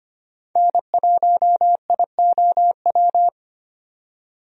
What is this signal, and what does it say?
Morse code, heard as a keyed tone, about 25 wpm, 700 Hz. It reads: N1IOW
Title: USA Call Signs - 5 Characters - 25wpm